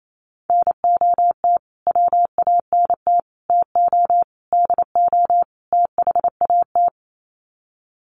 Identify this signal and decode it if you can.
Morse code, heard as a keyed tone, about 28 wpm, 700 Hz. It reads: NOT WANT TO DO THAT
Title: Sets of 5 Words Encore 28wpm